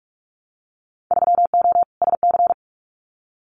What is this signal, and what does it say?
Morse code, heard as a keyed tone, about 44 wpm, 700 Hz. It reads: 3O SC
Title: Winter Field Day Exchanges Encore 45wpm